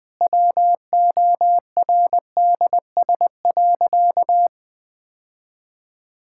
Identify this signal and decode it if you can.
Morse code, heard as a keyed tone, about 21 wpm, 700 Hz. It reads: WORDS.
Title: Easy Sentences with Hints - 20wpm